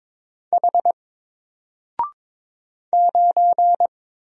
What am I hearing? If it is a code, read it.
Morse code, heard as a keyed tone, about 22 wpm, 700 Hz. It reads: H 9
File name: Single-Letter-Number-22wpm